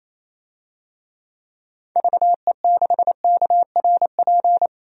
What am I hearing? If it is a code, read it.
Morse code, heard as a keyed tone, about 28 wpm, 700 Hz. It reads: VE6KRP
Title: DXCC-100 Call Signs - 6 Characters - 28wpm